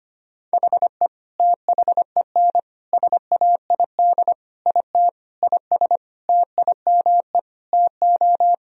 Morse code, heard as a keyed tone, about 25 wpm, 700 Hz. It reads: HE THEN SAID IT IS TIME TO
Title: Sentences from Top 1000 Words - Review Part 1 - 25wpm